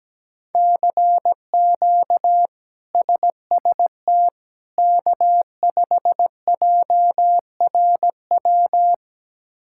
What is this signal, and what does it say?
Morse code, heard as a keyed tone, about 17 wpm, 700 Hz. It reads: CQ SST K5JRW
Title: K1USN SST QSO Practice - 17wpm